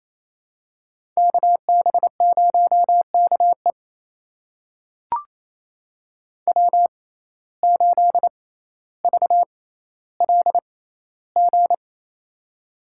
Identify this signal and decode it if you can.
Morse code, heard as a keyed tone, about 28 wpm, spaced as for 14 wpm, 700 Hz. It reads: KB0KE W8VLG
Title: USA Call Signs - 5 Characters - 6x Character Spacing - 28wpm